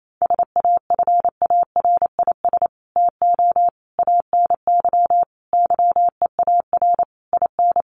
Morse code, heard as a keyed tone, about 28 wpm, 700 Hz. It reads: SAFARIS TO ANY YEAR IN
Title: A Sound Of Thunder 28wpm